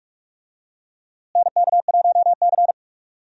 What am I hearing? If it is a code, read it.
Morse code, heard as a keyed tone, about 46 wpm, 700 Hz. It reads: NK1C
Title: USA Call Signs - 4 Characters - Repeated 2x - 45wpm